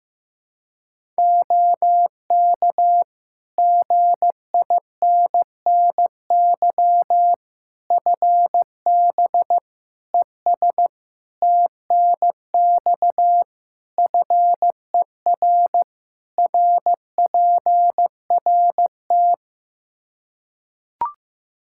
Morse code, heard as a keyed tone, about 15 wpm, 700 Hz. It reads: OK GINNY FB ES TNX FER RPRT
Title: Practice QSO - Exchange 2 - Info - 15wpm